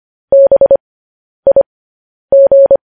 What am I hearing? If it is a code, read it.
Morse code, heard as a keyed tone, about 25 wpm, spaced as for 10 wpm, 550 Hz. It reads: BIG
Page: news headlines in morse code at 10 wpm